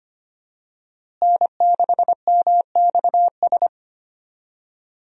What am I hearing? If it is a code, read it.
Morse code, heard as a keyed tone, about 25 wpm, 700 Hz. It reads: N6MXS
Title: USA Call Signs - 5 Characters - Repeated 3x - 25wpm